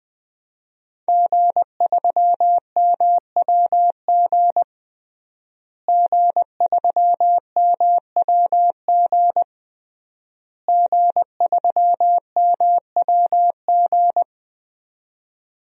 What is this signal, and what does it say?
Morse code, heard as a keyed tone, about 21 wpm, 700 Hz. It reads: G3MWG G3MWG G3MWG
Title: DXCC-100 Call Signs - 5 Characters - Repeated 3x - 20wpm